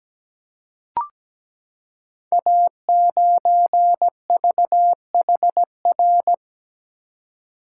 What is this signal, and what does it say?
Morse code, heard as a keyed tone, about 17 wpm, 700 Hz. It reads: A9VHR
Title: Code Groups - 5 Letter:Numbers - 17wpm